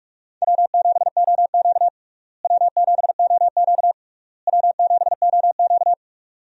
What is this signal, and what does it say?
Morse code, heard as a keyed tone, about 46 wpm, 700 Hz. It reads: W7OQ W7OQ W7OQ
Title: USA Call Signs - 4 Characters - Repeated 3x - 45wpm